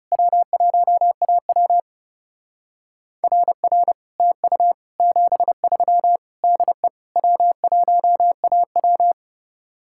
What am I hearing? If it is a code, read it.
Morse code, heard as a keyed tone, about 36 wpm, 700 Hz. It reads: W1AW RR TU 73 DE W1AW
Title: Speed-Racing Common QSO Phrases Encore - 30wpm